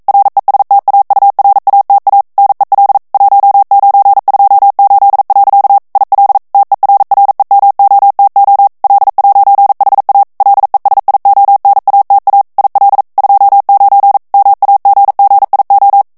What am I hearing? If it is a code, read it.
Morse code, heard as a keyed tone, about 42 wpm, 800 Hz. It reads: RESTAURATA NEL 1928. IL TERREMOTO L'HA LESIONATA IL 29 MAGGIO
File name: ITALIEN texte clair 1 vitesse 40 mots minute